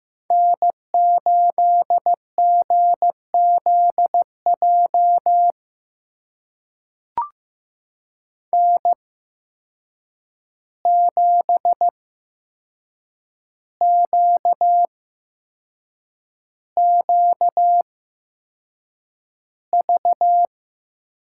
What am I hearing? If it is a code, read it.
Morse code, heard as a keyed tone, about 15 wpm, spaced as for 6 wpm, 700 Hz. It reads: N8GZJ N7QQV
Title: USA Call Signs - 5 Characters - 8x Character Spacing - 15wpm